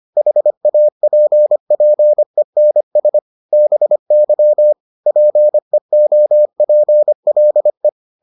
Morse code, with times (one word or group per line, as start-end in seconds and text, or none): 0.2-3.2 HAPPENS
3.5-4.7 BY
5.1-7.9 PEOPLE